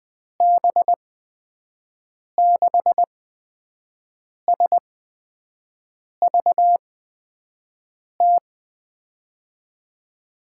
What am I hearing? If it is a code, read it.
Morse code, heard as a keyed tone, about 20 wpm, spaced as for 5 wpm, 700 Hz. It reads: B6SVT
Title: USA Call Signs - 6 Characters - 8x Character Spacing - 20wpm